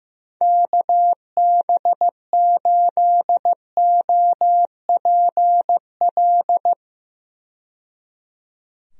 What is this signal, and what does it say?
Morse code, heard as a keyed tone, about 15 wpm, 700 Hz. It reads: KB8OPL
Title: USA Call Signs - 6 Characters - 15wpm